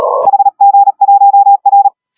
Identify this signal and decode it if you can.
Morse code, heard as a keyed tone, about 36 wpm, 800 Hz. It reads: HG1R
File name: AUDIO2019 / HG1R CW SSB MIX